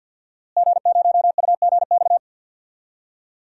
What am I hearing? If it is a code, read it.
Morse code, heard as a keyed tone, about 53 wpm, 700 Hz. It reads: G0UGX
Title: DXCC-100 Call Signs - 5 Characters - Repeated 2x - 50wpm